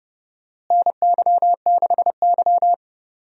Morse code, heard as a keyed tone, about 31 wpm, 700 Hz. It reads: NY6Y
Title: USA Call Signs - 4 Characters - Repeated 3x - 30wpm